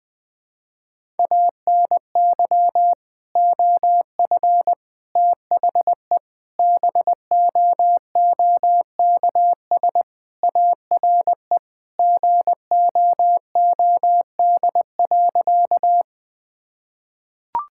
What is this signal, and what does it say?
Morse code, heard as a keyed tone, about 20 wpm, 700 Hz. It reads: ANY OF THE BOOKS ARE GOOD.
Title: Sentences with 3-Word Phrases - Spoken Beginning - 20wpm